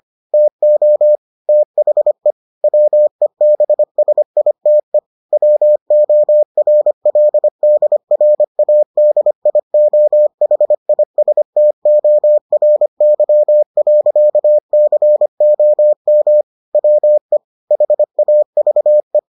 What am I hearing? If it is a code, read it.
Morse code, heard as a keyed tone, about 25 wpm, 600 Hz. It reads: TO THE WEBSITE WORLDRADIOHISTORY.COM WE HAVE